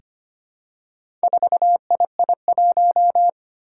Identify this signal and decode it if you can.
Morse code, heard as a keyed tone, about 25 wpm, 700 Hz. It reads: 4II1
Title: Four Challenging Letter-Number Combinations 25wpm